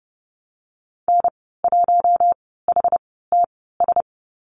Morse code, heard as a keyed tone, about 30 wpm, spaced as for 17 wpm, 700 Hz. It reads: N1HTS
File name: Lesson 15g - Number 5 - Call Signs 30wpm